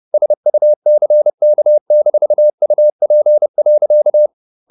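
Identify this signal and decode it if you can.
Morse code, heard as a keyed tone, about 30 wpm, 600 Hz. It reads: SUCK-UP.
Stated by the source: Morse code at 30 wpm, the novel Never by Ken Follett in English, from CW med Gnister.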